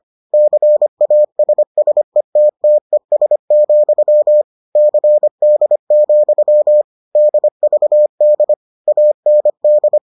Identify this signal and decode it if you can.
Morse code, heard as a keyed tone, about 25 wpm, 600 Hz. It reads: CASSETTES, CD, DVD AND